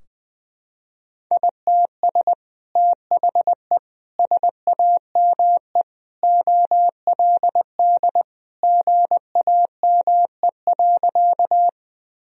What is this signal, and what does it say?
Morse code, heard as a keyed tone, about 19 wpm, 700 Hz. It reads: ITS THE SAME OLD GAME.
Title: Sentences from Top 1000 Words - Review Part 3 - 20wpm